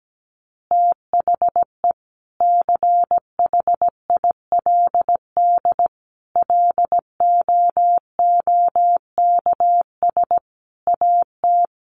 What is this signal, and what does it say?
Morse code, heard as a keyed tone, about 17 wpm, 700 Hz. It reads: THE CHILD LOOKS AT
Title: Additional Easy Phrases 17wpm